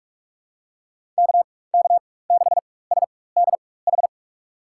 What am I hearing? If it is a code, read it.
Morse code, heard as a keyed tone, about 46 wpm, spaced as for 21 wpm, 690 Hz. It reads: KK6SDH
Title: USA Call Signs - 4x Character Spacing - 45wpm